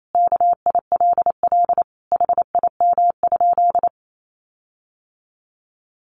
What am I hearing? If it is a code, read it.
Morse code, heard as a keyed tone, about 28 wpm, 700 Hz. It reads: KILL HIM?
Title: Sentences from Top 1400 Words 28wpm